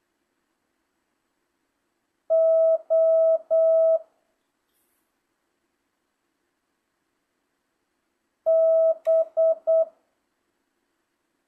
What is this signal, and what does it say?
Morse code, heard as a keyed tone, about 8 wpm, 650 Hz. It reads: O B